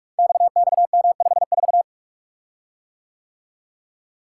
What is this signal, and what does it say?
Morse code, heard as a keyed tone, about 44 wpm, 700 Hz. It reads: XXM54
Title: Code Groups - 5 Letter:Numbers - 45wpm